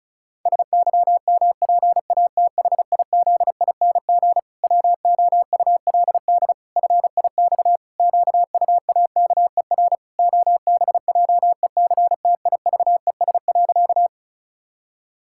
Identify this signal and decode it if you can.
Morse code, heard as a keyed tone, about 34 wpm, 700 Hz. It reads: SYMPATHIZING WOULD FIX QUAKER OBJECTIVES.
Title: Pangram Challenge v2 - 35wpm